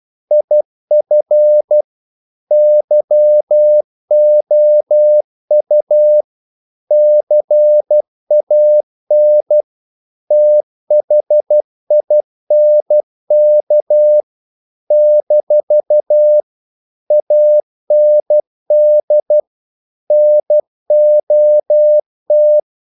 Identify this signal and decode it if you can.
Morse code, heard as a keyed tone, about 12 wpm, 600 Hz. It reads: IF YOU CAN THINK - AND NOT
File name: CW / Practice / If12